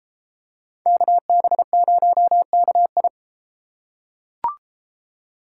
Morse code, heard as a keyed tone, about 32 wpm, 700 Hz. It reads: KB0KI